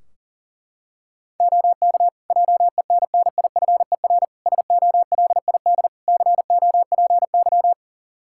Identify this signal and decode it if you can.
Morse code, heard as a keyed tone, about 37 wpm, 700 Hz. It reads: OK JENNIFER SOLID COPY
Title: Practice QSO - Exchange 3 - More Info - 40wpm